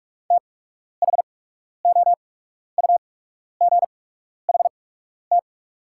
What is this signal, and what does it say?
Morse code, heard as a keyed tone, about 47 wpm, spaced as for 12 wpm, 700 Hz. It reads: THOUGHT
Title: 8x Character Spacing - Top 300 Words - 45wpm